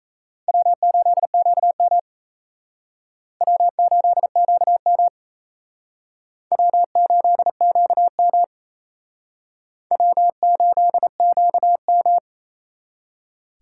Morse code, heard as a keyed tone, about 37 wpm, 700 Hz. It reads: W8QM W8QM W8QM W8QM